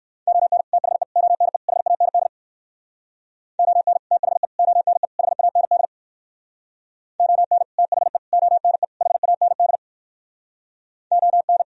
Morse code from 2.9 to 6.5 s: ON THE ONE HAND

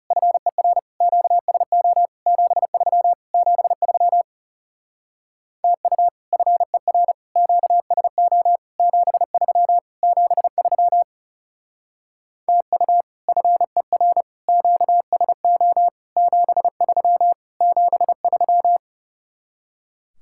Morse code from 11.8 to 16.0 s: TU FER QSO

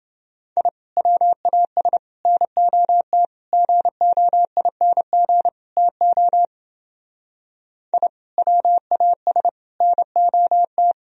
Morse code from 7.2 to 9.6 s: I WAS

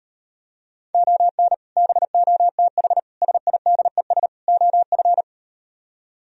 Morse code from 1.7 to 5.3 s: BOTH SIDES OF

